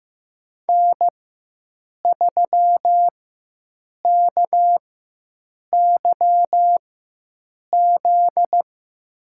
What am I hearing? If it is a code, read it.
Morse code, heard as a keyed tone, about 15 wpm, spaced as for 7 wpm, 700 Hz. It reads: N3KYZ
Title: USA Call Signs - 5 Characters - 4x Character Spacing - 15wpm